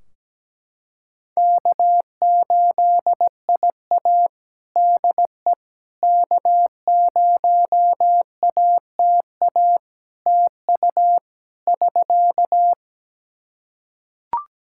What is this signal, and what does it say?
Morse code, heard as a keyed tone, about 17 wpm, 700 Hz. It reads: K8IA DE K0ATA TU <SK>
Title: Practice QSO - Exchange 4 - Ending - 17wpm